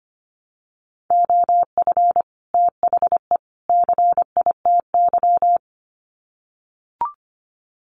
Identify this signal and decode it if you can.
Morse code, heard as a keyed tone, about 25 wpm, 700 Hz. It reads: OF THE CITY
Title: Sets of 3 Words 25wpm